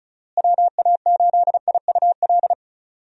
Morse code, heard as a keyed tone, about 36 wpm, 700 Hz. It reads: WA8IUL
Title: USA Call Signs - 6 Characters - Repeated 3x - 35wpm